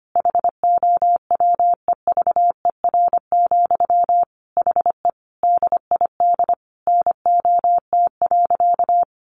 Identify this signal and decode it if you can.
Morse code, heard as a keyed tone, about 25 wpm, 700 Hz. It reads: HOWEVER, HE DID NOT.
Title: Sentences from Top 600 Words 25wpm